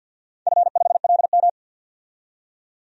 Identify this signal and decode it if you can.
Morse code, heard as a keyed tone, about 53 wpm, 700 Hz. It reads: F5LM
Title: DXCC-100 Call Signs - 4 Characters - 50wpm